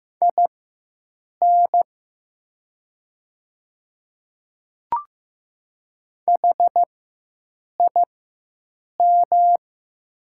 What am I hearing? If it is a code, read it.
Morse code, heard as a keyed tone, about 16 wpm, spaced as for 7 wpm, 700 Hz. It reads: IN HIM
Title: 4x Character Spacing - Top 100 Words - 15wpm